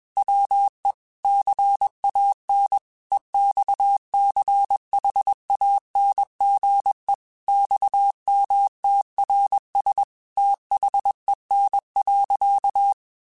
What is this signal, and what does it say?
Morse code, heard as a keyed tone, about 21 wpm, 800 Hz. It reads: WE CAN EXCHANGE XMTRS THEN.
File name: QSOs / QSOs-mp3s / WA7QOX 1@20